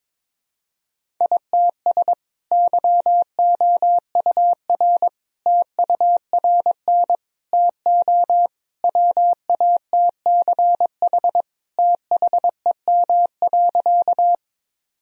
Morse code, heard as a keyed tone, about 22 wpm, 700 Hz. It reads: ITS YOUR TURN TO WATCH THEM.
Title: Sentences from Top 1000 Words - Review Part 1 - 22wpm